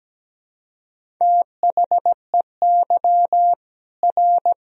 Morse code, heard as a keyed tone, about 17 wpm, 700 Hz. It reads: THEY R
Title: Sentences from Top 300 Words and Abbreviations 17wpm